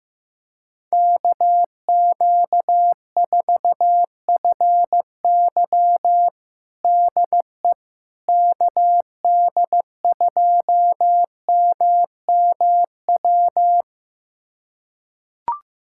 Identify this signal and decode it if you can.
Morse code, heard as a keyed tone, about 16 wpm, 700 Hz. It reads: KQ4FY DE KD2MMW
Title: USA Callsign DE Callsign 15wpm